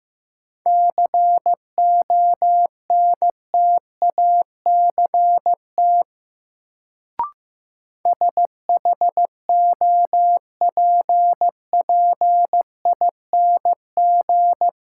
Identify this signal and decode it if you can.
Morse code, heard as a keyed tone, about 15 wpm, 700 Hz. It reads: CONTACT SHOPPING
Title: Sentences from Top 1000 Words - Review Part 7 - New Word Drill - 15wpm